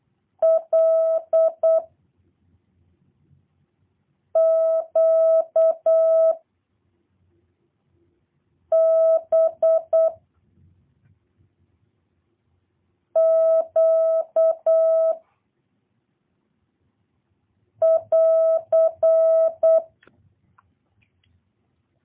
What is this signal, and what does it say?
Morse code, heard as a keyed tone, about 8 wpm, 660 Hz. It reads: L Q B Q <AR>
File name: jh8cbh / 2023cw / review1-6